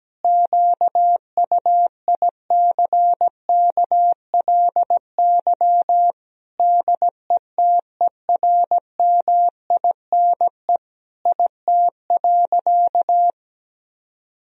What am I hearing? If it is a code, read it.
Morse code, heard as a keyed tone, about 17 wpm, 700 Hz. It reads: QUICKLY DETERMINE IT.
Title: Sentences from Top 1000 Words - Review Part 4 - 17wpm